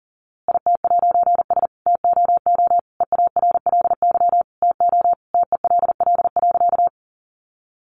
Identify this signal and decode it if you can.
Morse code, heard as a keyed tone, about 40 wpm, 700 Hz. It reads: IT'S TOO EARLY TO TELL.
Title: A Sound Of Thunder 40wpm